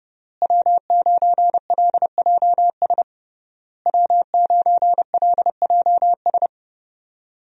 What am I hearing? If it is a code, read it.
Morse code, heard as a keyed tone, about 30 wpm, 700 Hz. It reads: W9LJS W9LJS
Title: USA Call Signs - 5 Characters - Repeated 3x - 30wpm